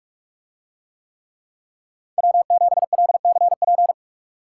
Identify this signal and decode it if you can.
Morse code, heard as a keyed tone, about 44 wpm, 700 Hz. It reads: W7LCP